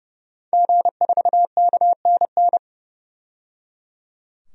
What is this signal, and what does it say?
Morse code, heard as a keyed tone, about 30 wpm, 700 Hz. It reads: G4KNN
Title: DXCC-100 Call Signs - 5 Characters - 30wpm